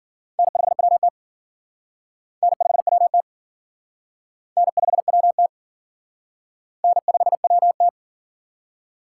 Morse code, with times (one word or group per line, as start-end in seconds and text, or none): 0.4-1.1 N5WT
2.4-3.2 N5WT
4.6-5.5 N5WT
6.8-7.9 N5WT